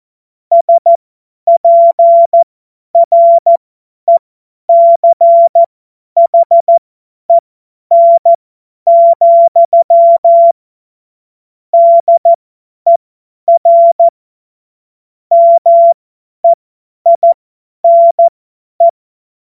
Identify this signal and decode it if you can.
Morse code, heard as a keyed tone, about 14 wpm, spaced as for 10 wpm, 680 Hz. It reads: SPRECHEN, DER MEINE
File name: Witz-02-September-2025-50-F70